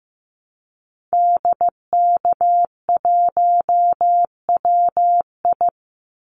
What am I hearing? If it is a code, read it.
Morse code, heard as a keyed tone, about 15 wpm, 700 Hz. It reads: DK1WI
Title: Call Signs Repeated 2x - 15wpm